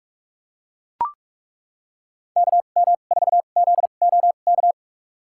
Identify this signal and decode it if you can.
Morse code, heard as a keyed tone, about 41 wpm, spaced as for 31 wpm, 700 Hz. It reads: KM4ZOK
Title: USA Call Signs - 6 Characters - 2x Character Spacing - 45wpm